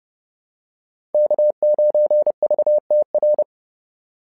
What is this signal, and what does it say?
Morse code, heard as a keyed tone, about 30 wpm, 600 Hz. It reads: K9VTR
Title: SOTA QSO Practice 30wpm